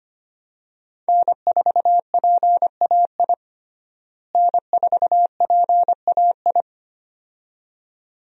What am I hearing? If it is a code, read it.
Morse code, heard as a keyed tone, about 25 wpm, 700 Hz. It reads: N4PAI N4PAI
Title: USA Call Signs - 5 Characters - Repeated 2x - 25wpm